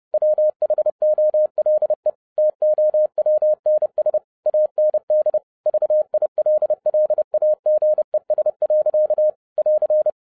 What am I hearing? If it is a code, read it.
Morse code, heard as a keyed tone, about 30 wpm, 610 Hz. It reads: WHOLE TOWNS AND VILLAGES. <AR>